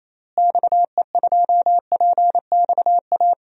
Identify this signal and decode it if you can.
Morse code, heard as a keyed tone, about 28 wpm, 700 Hz. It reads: XE2PXA
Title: DXCC-100 Call Signs - 6 Characters - Repeated 3x - 28wpm